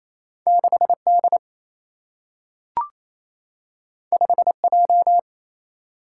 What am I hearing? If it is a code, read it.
Morse code, heard as a keyed tone, about 28 wpm, 700 Hz. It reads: 6D 5J
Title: Single-Letter Single-Number 28wpm